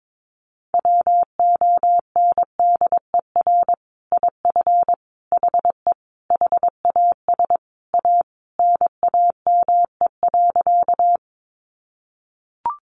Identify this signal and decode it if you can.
Morse code, heard as a keyed tone, about 22 wpm, 700 Hz. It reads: WONDER IF HE HAS A NAME.
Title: Sentences from Top 500 Words 22wpm